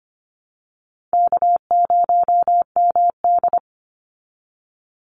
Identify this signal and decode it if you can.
Morse code, heard as a keyed tone, about 25 wpm, 700 Hz. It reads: K0MD
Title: Call Signs Repeated 2x - 25wpm